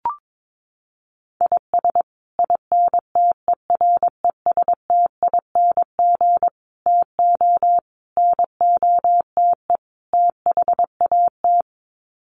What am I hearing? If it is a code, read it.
Morse code, heard as a keyed tone, about 22 wpm, 700 Hz. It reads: IS INTERESTING TO NOTE THAT
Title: Sets of 5 Words Encore2 22wpm